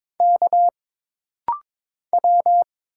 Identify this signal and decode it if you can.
Morse code, heard as a keyed tone, about 22 wpm, 700 Hz. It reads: K W
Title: Single Letter-Number - Rapid-Fire 22wpm